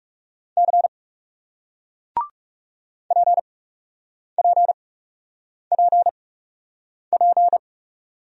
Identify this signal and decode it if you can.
Morse code, heard as a keyed tone, about 38 wpm, spaced as for 8 wpm, 700 Hz. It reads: C PPPP